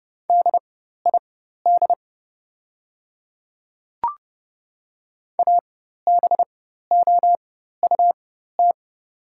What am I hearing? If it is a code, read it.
Morse code, heard as a keyed tone, about 30 wpm, spaced as for 14 wpm, 700 Hz. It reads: DID ABOUT
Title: 4x Character Spacing - Top 100 Words - 30wpm